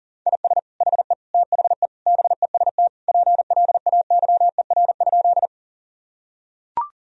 Morse code, heard as a keyed tone, about 40 wpm, 700 Hz. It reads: IS HE THE BEST PLAYER?
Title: Sentences from Top 1000 Words - Review Part 4 - 40wpm